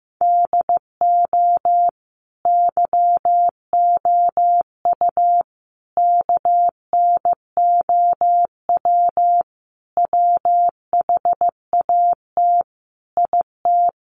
Morse code, heard as a keyed tone, about 15 wpm, 700 Hz. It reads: DO YOU KNOW WHAT IT
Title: Sentences from Top 100 Words 15wpm